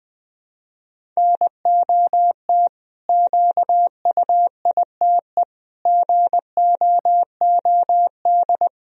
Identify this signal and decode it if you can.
Morse code, heard as a keyed tone, about 20 wpm, 700 Hz. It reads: NOT QUITE GOOD
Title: Sentences from Top 1000 Words - Review Part 1 - 20wpm